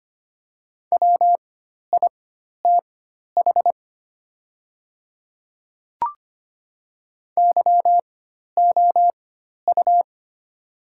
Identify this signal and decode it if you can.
Morse code, heard as a keyed tone, about 25 wpm, spaced as for 12 wpm, 700 Hz. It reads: WITH YOU
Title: 4x Character Spacing - Top 200 Words - 25wpm